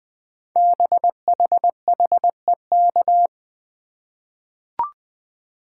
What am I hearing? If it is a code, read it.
Morse code, heard as a keyed tone, about 21 wpm, 700 Hz. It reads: BHHEK
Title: Code Groups - 5 Letters - 20wpm